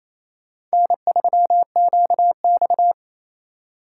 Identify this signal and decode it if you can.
Morse code, heard as a keyed tone, about 34 wpm, 700 Hz. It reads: N3QX
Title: Speed Racing - USA Call Signs 28wpm